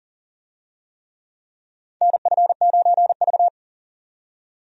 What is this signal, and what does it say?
Morse code, heard as a keyed tone, about 40 wpm, 700 Hz. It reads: NF9V
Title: USA Call Signs - 4 Characters - 40wpm